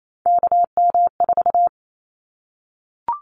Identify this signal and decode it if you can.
Morse code, heard as a keyed tone, about 28 wpm, 700 Hz. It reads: KM4
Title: Callsign Prefixes for Top 25 DXCC - 28wpm